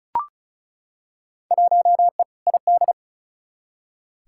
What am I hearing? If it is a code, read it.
Morse code, heard as a keyed tone, about 34 wpm, 700 Hz. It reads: 1E ID
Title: Field Day Exchanges - 35wpm